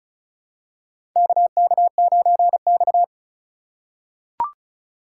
Morse code, heard as a keyed tone, about 36 wpm, 700 Hz. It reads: KK9X
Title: USA Call Signs - 2x Character Spacing - 35wpm